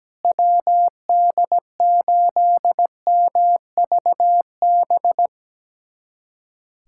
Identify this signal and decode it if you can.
Morse code, heard as a keyed tone, about 17 wpm, 700 Hz. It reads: WD8MVB
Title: USA Callsign DE Callsign 17wpm